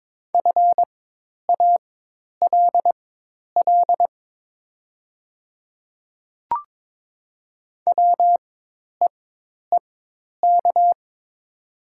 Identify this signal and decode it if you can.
Morse code, heard as a keyed tone, about 22 wpm, spaced as for 10 wpm, 700 Hz. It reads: FALL WEEK
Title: 4x Character Spacing - Top 500 Words - 22wpm